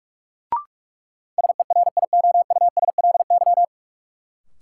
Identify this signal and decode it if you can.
Morse code, heard as a keyed tone, about 46 wpm, 700 Hz. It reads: SERIOUSLY
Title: Sentences from Top 1200 Words - New Word Drill Alt - 45wpm